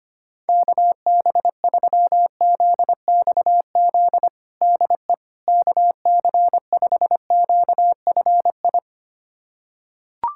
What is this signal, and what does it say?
Morse code, heard as a keyed tone, about 25 wpm, 700 Hz. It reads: KB3ZXZ DE KC5QFI
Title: USA Callsign DE Callsign - 6 Characters - 25wpm